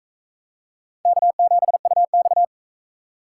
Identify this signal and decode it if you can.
Morse code, heard as a keyed tone, about 42 wpm, 700 Hz. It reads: K7UX